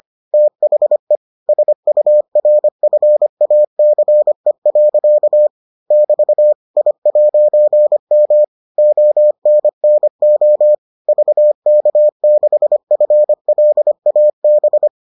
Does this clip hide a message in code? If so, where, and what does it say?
Yes: Morse code from 0.3 to 14.9 s, heard as a keyed tone, about 25 wpm, 600 Hz. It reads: THE SURFACE. <BT> I'M ONNO VK6FLAB